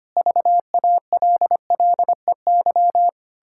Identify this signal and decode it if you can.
Morse code, heard as a keyed tone, about 25 wpm, 700 Hz. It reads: VALLEY
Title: Pangram Challenge v2 - 25wpm